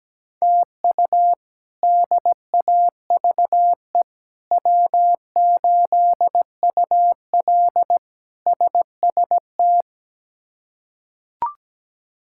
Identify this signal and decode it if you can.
Morse code, heard as a keyed tone, about 17 wpm, 700 Hz. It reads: TU DAVE W8UL SST
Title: K1USN SST QSO Practice - 17wpm